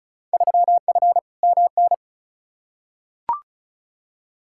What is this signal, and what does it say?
Morse code, heard as a keyed tone, about 34 wpm, 700 Hz. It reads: 3F MN
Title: Field Day Exchanges Encore - 35wpm